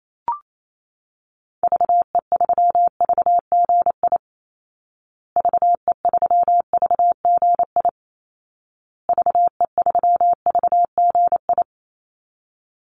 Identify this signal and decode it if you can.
Morse code, heard as a keyed tone, about 28 wpm, 700 Hz. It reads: VE3VGI VE3VGI VE3VGI
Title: Call Signs Repeated 2x - 28wpm